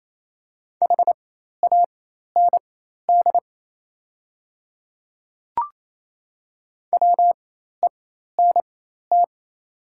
Morse code, heard as a keyed tone, about 28 wpm, spaced as for 13 wpm, 690 Hz. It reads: HAND WENT
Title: 4x Character Spacing - Top 300 Words - 28wpm